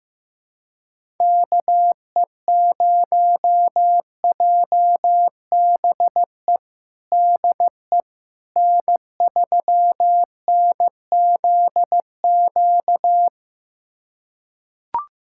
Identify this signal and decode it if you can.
Morse code, heard as a keyed tone, about 16 wpm, 700 Hz. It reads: KE0JBE DE N3NZQ
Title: USA Callsign DE Callsign 15wpm